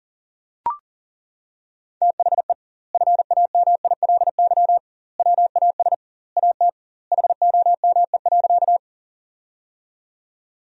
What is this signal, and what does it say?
Morse code, heard as a keyed tone, about 40 wpm, 700 Hz. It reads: THE FAMILY WAS AT HOME.
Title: 2x Word-Spacing - Sentences from Top 400 Words - 40wpm